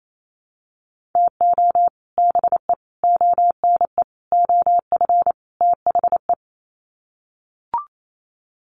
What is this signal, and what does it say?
Morse code, heard as a keyed tone, about 28 wpm, 700 Hz. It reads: TO BE ONE OF THE
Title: Sets of 5 Words 28wpm